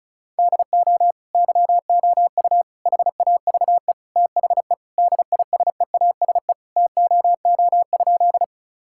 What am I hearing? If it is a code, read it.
Morse code, heard as a keyed tone, about 36 wpm, 700 Hz. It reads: DO YOU HAVE THE DISEASE TOO?
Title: Sentences from Top 1000 Words - Review Part 6 - 35wpm